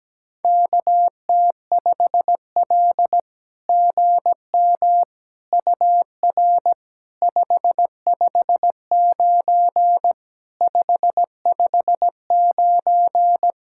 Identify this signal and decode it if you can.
Morse code, heard as a keyed tone, about 17 wpm, 700 Hz. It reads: KT5L GM UR 559 559
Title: SOTA QSO Practice 17wpm